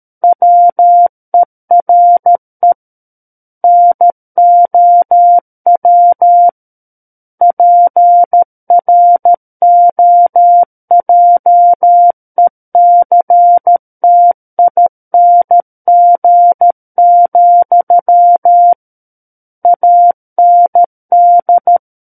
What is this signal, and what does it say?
Morse code, heard as a keyed tone, about 13 wpm, 700 Hz. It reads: WERE NOW PROJECTING, AND
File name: Book One / 04-Chapter 4 - 13 WPM